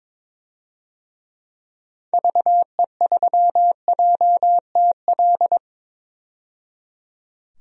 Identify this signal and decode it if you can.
Morse code, heard as a keyed tone, about 21 wpm, 700 Hz. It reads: VE3JTL
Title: DXCC-100 Call Signs - 6 Characters - 22wpm